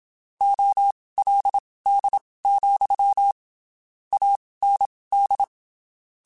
Morse code, heard as a keyed tone, about 27 wpm, spaced as for 19 wpm, 800 Hz. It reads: OLD, AND
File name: (spaced) / QSOs Spaced-mp3s / WA7QOX 2 @25 sp